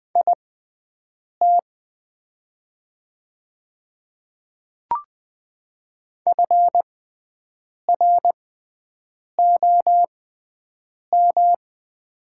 Morse code, heard as a keyed tone, about 20 wpm, spaced as for 7 wpm, 700 Hz. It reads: IT FROM
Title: 6x Character Spacing - Top 100 Words - 20wpm